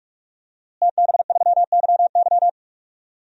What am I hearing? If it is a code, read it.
Morse code, heard as a keyed tone, about 46 wpm, 700 Hz. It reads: TB3YY
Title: DXCC-100 Call Signs - 5 Characters - Repeated 2x - 45wpm